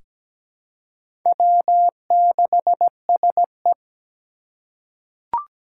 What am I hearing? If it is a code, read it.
Morse code, heard as a keyed tone, about 17 wpm, 700 Hz. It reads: W6SE
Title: USA Call Signs - 4 Characters - 3x Character Spacing - 17wpm